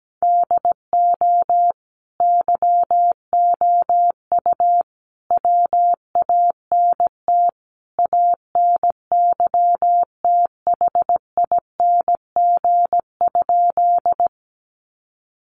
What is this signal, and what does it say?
Morse code, heard as a keyed tone, about 17 wpm, 700 Hz. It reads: DO YOU WANT ANYTHING?
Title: Sentences from Top 1000 Words 17wpm